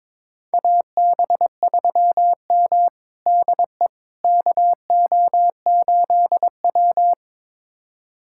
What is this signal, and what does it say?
Morse code, heard as a keyed tone, about 22 wpm, 700 Hz. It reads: AB3M DE KO8W
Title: USA Callsign DE Callsign - 4 Characters - 22wpm